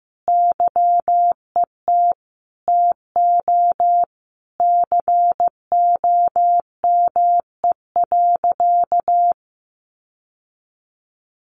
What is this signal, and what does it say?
Morse code, heard as a keyed tone, about 15 wpm, 700 Hz. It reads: YET TO COME.
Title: Sentences from Top 1500 Words 15wpm